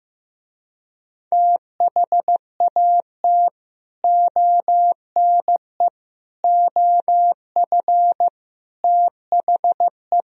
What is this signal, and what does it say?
Morse code, heard as a keyed tone, about 15 wpm, 700 Hz. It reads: THAT ONE OF THE
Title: Speed-Racing - Sets of 4 Words 15wpm